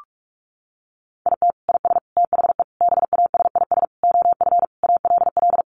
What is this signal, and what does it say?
Morse code, heard as a keyed tone, about 44 wpm, 700 Hz. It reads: IT IS THE BASIS OF ALL
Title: Sentences from Top 1200 Words 45wpm